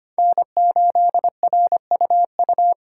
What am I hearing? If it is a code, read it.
Morse code, heard as a keyed tone, about 25 wpm, 700 Hz. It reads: N8RUU
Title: USA Call Signs - 5 Characters - 2x Character Spacing - 25wpm